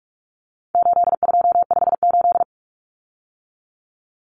Morse code, heard as a keyed tone, about 44 wpm, 700 Hz. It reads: 8258
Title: Common Cut Numbers - 4-Digit 45wpm